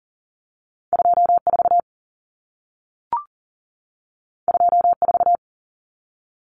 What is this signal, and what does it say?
Morse code, heard as a keyed tone, about 40 wpm, 700 Hz. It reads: 24 24
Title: Numbers - 2-Digit 40wpm